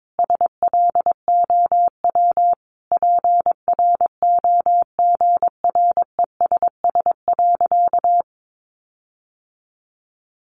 Morse code, heard as a keyed tone, about 22 wpm, 700 Hz. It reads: SLOW PROGRESS.
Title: Sentences from Top 1400 Words 22wpm